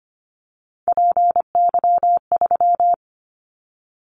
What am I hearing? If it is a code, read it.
Morse code, heard as a keyed tone, about 25 wpm, 700 Hz. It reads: PY3
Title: Callsign Prefixes for Top 50 DXCC - 25wpm